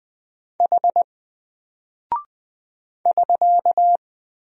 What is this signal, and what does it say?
Morse code, heard as a keyed tone, about 20 wpm, 700 Hz. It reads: H <SK>
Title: Single Character (Letter:Number:Punctuation:Prosign) Alt 20wpm